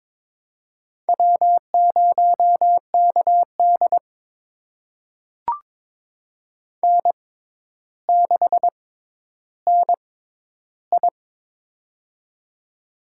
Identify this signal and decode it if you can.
Morse code, heard as a keyed tone, about 22 wpm, spaced as for 11 wpm, 700 Hz. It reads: W0KD N6NI
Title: USA Call Signs - 4 Characters - 6x Character Spacing - 22wpm